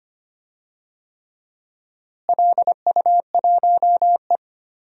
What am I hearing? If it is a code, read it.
Morse code, heard as a keyed tone, about 25 wpm, 700 Hz. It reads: LU1E
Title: DXCC-100 Call Signs - 4 Characters - Repeated 3x - 25wpm